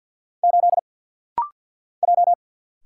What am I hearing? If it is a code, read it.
Morse code, heard as a keyed tone, about 49 wpm, 700 Hz. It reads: 8 J
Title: Single Letter-Number - Rapid-Fire 50wpm